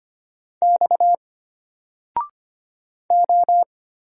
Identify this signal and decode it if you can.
Morse code, heard as a keyed tone, about 25 wpm, 700 Hz. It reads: X O
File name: Single-Letter-Number-25wpm